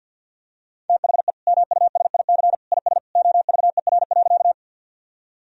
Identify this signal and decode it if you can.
Morse code, heard as a keyed tone, about 50 wpm, 700 Hz. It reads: THE MUSIC IS OVER.
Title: Sentences from Top 1000 Words - Review Part 1 - 50wpm